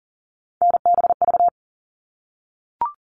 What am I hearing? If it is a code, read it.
Morse code, heard as a keyed tone, about 40 wpm, 700 Hz. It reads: NBV
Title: All Cut Numbers - 3-Digit 40wpm